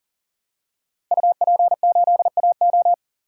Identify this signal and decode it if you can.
Morse code, heard as a keyed tone, about 40 wpm, 700 Hz. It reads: UP8AO
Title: Code Groups - 5 Letter:Numbers - 40wpm